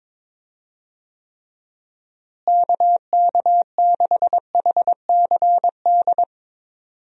Morse code, heard as a keyed tone, about 22 wpm, 700 Hz. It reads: KK6HCD
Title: USA Call Signs - 6 Characters - Repeated 3x - 22wpm